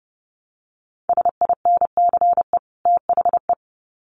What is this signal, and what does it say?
Morse code, heard as a keyed tone, about 30 wpm, 700 Hz. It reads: SINCE THE
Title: Sets of 2 Words 30wpm